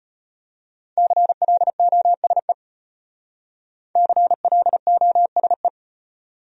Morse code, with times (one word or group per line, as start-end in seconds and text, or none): 1.0-2.5 CLOSE
4.0-5.7 CLOSE